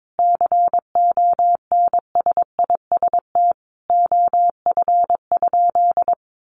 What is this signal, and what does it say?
Morse code, heard as a keyed tone, about 22 wpm, 700 Hz. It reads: CONSIST OF?
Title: Sentences from Top 1400 Words 22wpm